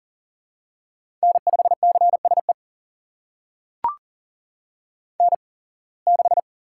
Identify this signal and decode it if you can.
Morse code, heard as a keyed tone, about 38 wpm, spaced as for 26 wpm, 700 Hz. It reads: N5CSE N6
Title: USA Call Signs - 5 Characters - 8x Character Spacing - 40wpm